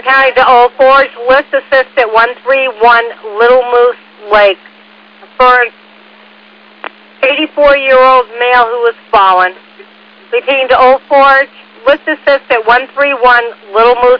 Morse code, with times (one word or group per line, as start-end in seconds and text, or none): none